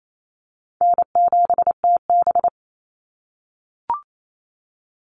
Morse code, heard as a keyed tone, about 28 wpm, 700 Hz. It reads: N7TB
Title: Call Signs - 28wpm